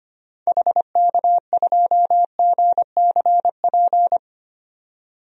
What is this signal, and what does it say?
Morse code, heard as a keyed tone, about 25 wpm, 700 Hz. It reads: HK2GCP
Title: DXCC-100 Call Signs - 6 Characters - 25wpm